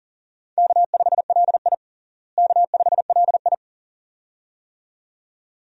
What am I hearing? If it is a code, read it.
Morse code, heard as a keyed tone, about 40 wpm, 700 Hz. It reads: K5LI K5LI
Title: USA Call Signs - 4 Characters - Repeated 2x - 40wpm